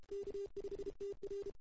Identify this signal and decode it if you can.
Morse code, heard as a keyed tone, about 29 wpm, 410 Hz. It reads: K5TR
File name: CW / 2008-ss-cw-k5tr-op-n5rz -02:00